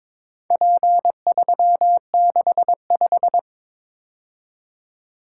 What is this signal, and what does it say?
Morse code, heard as a keyed tone, about 22 wpm, 700 Hz. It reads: P365
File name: practice / 22wpm / Four Letter-Number Combinations - 22wpm